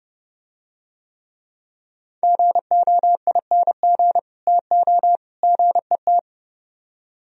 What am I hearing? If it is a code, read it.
Morse code, heard as a keyed tone, about 30 wpm, 700 Hz. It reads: GOING TO GET